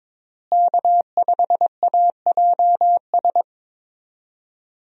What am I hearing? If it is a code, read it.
Morse code, heard as a keyed tone, about 22 wpm, 700 Hz. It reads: K5AJS
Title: USA Call Signs - 5 Characters - Repeated 3x - 22wpm